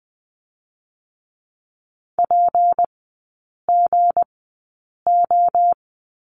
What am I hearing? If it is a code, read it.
Morse code, heard as a keyed tone, about 20 wpm, spaced as for 8 wpm, 690 Hz. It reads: PGO
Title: 159 Lesson 26e - Letter g - Three Characters 20wpm